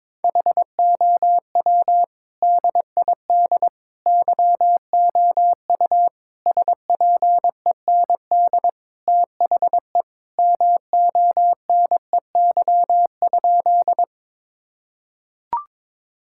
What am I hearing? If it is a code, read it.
Morse code, heard as a keyed tone, about 22 wpm, 700 Hz. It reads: HOW DID YOU SPEND THE MONEY?
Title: Sentences from Top 1000 Words - Review Part 5 - 22wpm